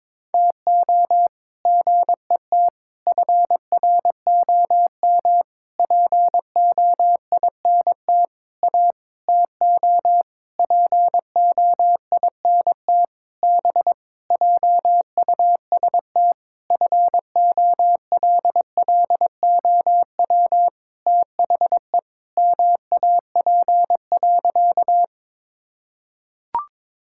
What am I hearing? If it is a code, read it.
Morse code, heard as a keyed tone, about 22 wpm, 700 Hz. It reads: TO GET FROM POINT A TO POINT B JUST FOLLOW THE MAP.
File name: practice / 22wpm / Sentences with 8-Word Phrases - 22wpm